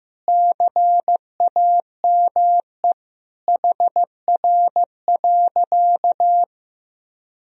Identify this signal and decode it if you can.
Morse code, heard as a keyed tone, about 15 wpm, 700 Hz. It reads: CAME HR.
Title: Sentences from Top 500 Words and Abbreviations 15wpm